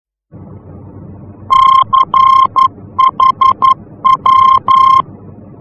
Morse code, heard as a keyed tone, about 11 wpm, 1060 Hz. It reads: CHW
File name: france / mx / CHW-F-VOR